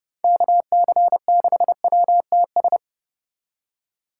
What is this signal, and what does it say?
Morse code, heard as a keyed tone, about 29 wpm, 700 Hz. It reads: KC6WTS